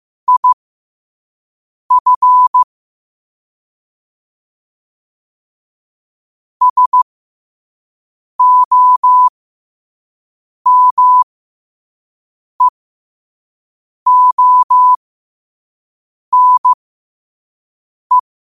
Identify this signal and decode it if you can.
Morse code, heard as a keyed tone, about 15 wpm, spaced as for 5 wpm, 1000 Hz. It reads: IF SOMEONE
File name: qotdfiles / 5wpm